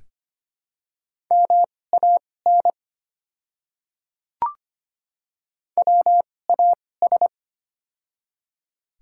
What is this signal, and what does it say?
Morse code, heard as a keyed tone, about 25 wpm, spaced as for 18 wpm, 700 Hz. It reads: MAN WAS
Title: 2x Character Spacing - Top 200 Words - 25wpm